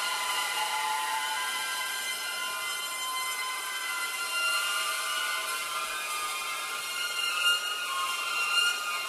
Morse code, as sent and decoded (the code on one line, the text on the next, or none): none